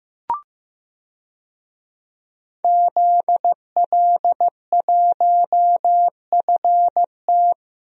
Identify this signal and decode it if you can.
Morse code, heard as a keyed tone, about 16 wpm, 700 Hz. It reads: ZL1FT
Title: DXCC-100 Call Signs - 5 Characters - Repeated 2x - 15wpm